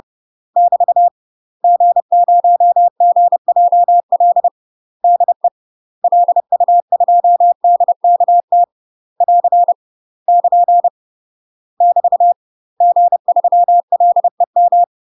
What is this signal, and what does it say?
Morse code, heard as a keyed tone, about 30 wpm, 700 Hz. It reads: <BT> G0GJL DE LU2DKT <AR> <KN> <BT> G3LEM
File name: UK MASTER QSOs TRAINING / 30WPM / UK MASTER QSOs 0022